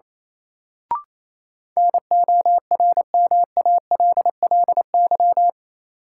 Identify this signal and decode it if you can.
Morse code, heard as a keyed tone, about 28 wpm, 700 Hz. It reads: NORMALLY
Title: Sentences from Top 1000 Words - Review Part 6 - New Word Drill - 28wpm